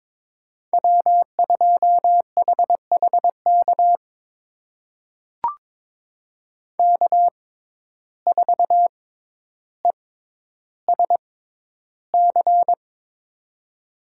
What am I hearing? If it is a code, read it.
Morse code, heard as a keyed tone, about 22 wpm, spaced as for 11 wpm, 700 Hz. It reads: W2HHK K4ESC
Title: USA Call Signs - 5 Characters - 6x Character Spacing - 22wpm